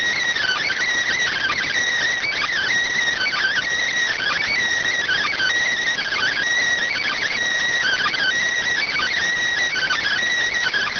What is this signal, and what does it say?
Emergency Alert System (SAME) tones (header bursts), heard as harsh buzzing bursts. It is heard from 1.1 to 10.2 s.